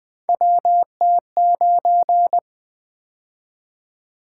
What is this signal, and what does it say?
Morse code, heard as a keyed tone, about 20 wpm, 700 Hz. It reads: WT9
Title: Three Letter-Number Combinations - 20wpm